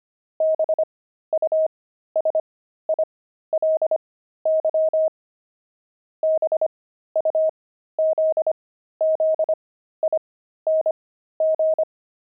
Morse code, heard as a keyed tone, about 25 wpm, spaced as for 13 wpm, 630 Hz. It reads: BUSILY BUZZING
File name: pangram-fs-13-cs-25-beehive